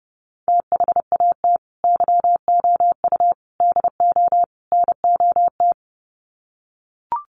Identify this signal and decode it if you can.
Morse code, heard as a keyed tone, about 30 wpm, 700 Hz. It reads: THAT YOU DO NOT
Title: Sets of 4 Words 30wpm